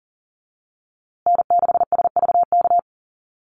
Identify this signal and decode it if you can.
Morse code, heard as a keyed tone, about 40 wpm, 700 Hz. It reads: N6SVK
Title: USA Call Signs Repeated 2x 40wpm